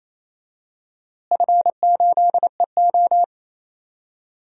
Morse code, heard as a keyed tone, about 28 wpm, 700 Hz. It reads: F8EO
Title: DXCC-100 Call Signs - 4 Characters - Repeated 3x - 28wpm